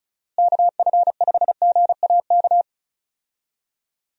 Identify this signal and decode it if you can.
Morse code, heard as a keyed tone, about 34 wpm, 700 Hz. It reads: KF5GAK
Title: USA Call Signs - 6 Characters - Repeated 3x - 35wpm